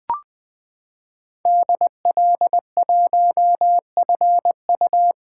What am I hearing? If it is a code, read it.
Morse code, heard as a keyed tone, about 20 wpm, 700 Hz. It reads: DL1FU
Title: 2023 SOTA Chaser Call Signs - 20wpm